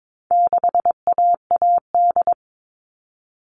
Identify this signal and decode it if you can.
Morse code, heard as a keyed tone, about 22 wpm, 700 Hz. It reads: 6AAD
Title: All Cut Numbers - 4-Digit 22wpm